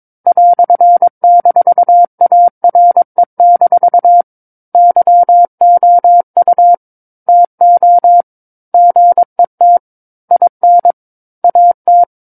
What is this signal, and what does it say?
Morse code, heard as a keyed tone, about 22 wpm, 700 Hz. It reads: "-ARE- YOU TO GET IN AT
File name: 06-Chapter-06 22 WPM